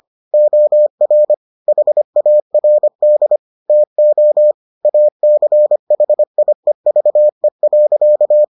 Morse code, heard as a keyed tone, about 25 wpm, 600 Hz. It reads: OR HARD TO ACHIEVE.